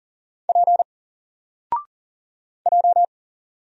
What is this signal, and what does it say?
Morse code, heard as a keyed tone, about 40 wpm, 700 Hz. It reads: P J
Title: Single Letter-Number Alt 40wpm